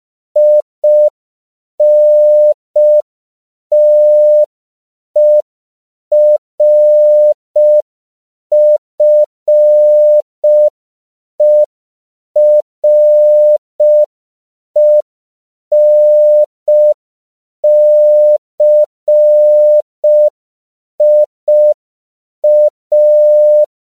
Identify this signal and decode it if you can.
Morse code, heard as a keyed tone, about 5 wpm, 600 Hz. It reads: INTERFERENCIA